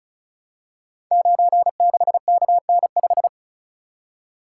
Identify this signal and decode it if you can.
Morse code, heard as a keyed tone, about 36 wpm, 700 Hz. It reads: 96KN5
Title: Code Groups - 5 Letter:Numbers - 35wpm